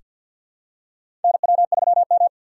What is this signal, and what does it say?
Morse code, heard as a keyed tone, about 45 wpm, 700 Hz. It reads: NW3M
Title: USA Call Signs - 4x Character Spacing - 50wpm